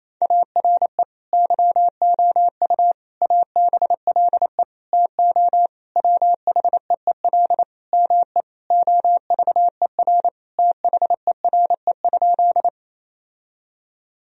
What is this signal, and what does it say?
Morse code, heard as a keyed tone, about 28 wpm, 700 Hz. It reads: ARE YOU ABLE TO WHEEL ME OVER THERE?
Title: Sentences from Top 1000 Words - Review Part 3 - 28wpm